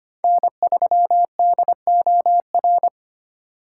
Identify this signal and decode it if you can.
Morse code, heard as a keyed tone, about 25 wpm, 700 Hz. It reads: N3DOR
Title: USA Call Signs - 5 Characters - 4x Character Spacing - 25wpm